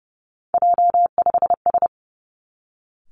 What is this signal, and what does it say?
Morse code, heard as a keyed tone, about 31 wpm, 700 Hz. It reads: J5S
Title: Three Challenging Letter-Number Combinations 30wpm